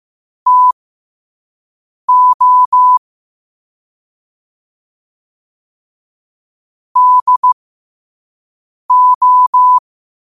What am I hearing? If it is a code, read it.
Morse code, heard as a keyed tone, about 15 wpm, spaced as for 5 wpm, 1000 Hz. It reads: TO DO
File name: qotdfiles / 5wpm